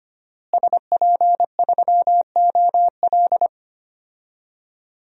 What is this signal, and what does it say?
Morse code, heard as a keyed tone, about 25 wpm, 700 Hz. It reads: SP3OL
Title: DXCC-100 Call Signs - 5 Characters - Repeated 2x - 25wpm